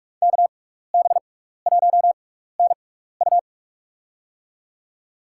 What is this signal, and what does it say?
Morse code, heard as a keyed tone, about 42 wpm, spaced as for 15 wpm, 700 Hz. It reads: KB1NU
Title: USA Call Signs - 5 Characters - 6x Character Spacing - 45wpm